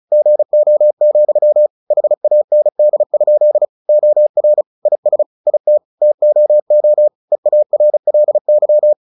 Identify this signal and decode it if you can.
Morse code, heard as a keyed tone, about 34 wpm, 600 Hz. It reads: GO, HAND? OR IS IT TOO EARLY
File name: the old man and the sea / the old man and the sea 1 35wpm